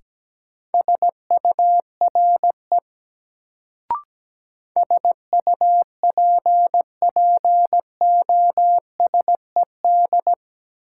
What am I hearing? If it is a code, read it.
Morse code, heard as a keyed tone, about 17 wpm, 700 Hz. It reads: SURE SUPPOSED
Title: Sets of 3 Words - New Word Drill - 17wpm